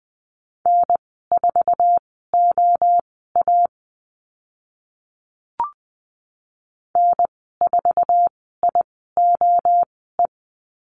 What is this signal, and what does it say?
Morse code, heard as a keyed tone, about 20 wpm, spaced as for 15 wpm, 700 Hz. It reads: N4OA N4IOE
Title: 062 Lesson 11g - Letter h - Call Signs 20wpm